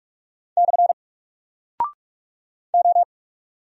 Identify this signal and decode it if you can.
Morse code, heard as a keyed tone, about 46 wpm, 700 Hz. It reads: / O
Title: Single Character (Letter:Number:Punctuation:Prosign) 45wpm